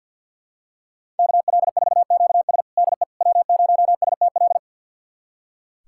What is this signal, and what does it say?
Morse code, heard as a keyed tone, about 46 wpm, 700 Hz. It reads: KL4QS DE W0STL
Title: USA Callsign DE Callsign - 5 Characters - 50wpm